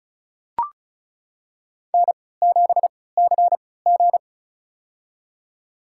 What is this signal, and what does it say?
Morse code, heard as a keyed tone, about 34 wpm, spaced as for 20 wpm, 700 Hz. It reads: N7CG
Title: USA Call Signs - 4 Characters - 3x Character Spacing - 35wpm